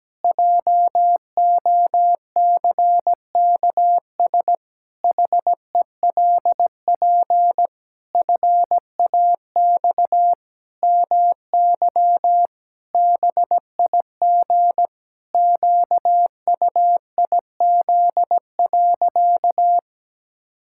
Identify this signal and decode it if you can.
Morse code, heard as a keyed tone, about 17 wpm, 700 Hz. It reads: JOCKS HELP FAX MY BIG QUIZ.
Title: Pangram Challenge v2 - 17wpm